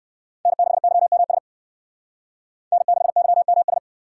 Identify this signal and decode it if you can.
Morse code, heard as a keyed tone, about 58 wpm, 700 Hz. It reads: N6YMS N6YMS